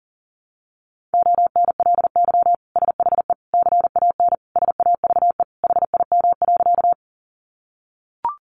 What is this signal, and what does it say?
Morse code, heard as a keyed tone, about 37 wpm, 700 Hz. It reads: ONLY SHE CAN SAVE HIM.
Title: Sentences from Top 1100 Words 40wpm